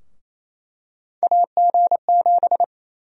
Morse code, heard as a keyed tone, about 27 wpm, 700 Hz. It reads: AG7
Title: USA Call Sign Prefixes - Repeated 2x - 28wpm